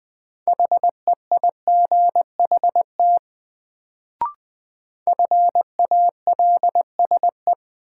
Morse code, heard as a keyed tone, about 21 wpm, 700 Hz. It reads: HEIGHT FALSE
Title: Sentences from Top 1600 Words - New Word Drill - 20wpm